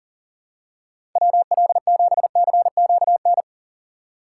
Morse code, heard as a keyed tone, about 40 wpm, 700 Hz. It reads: WL7CQN
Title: USA Call Signs - 6 Characters - 2x Character Spacing - 40wpm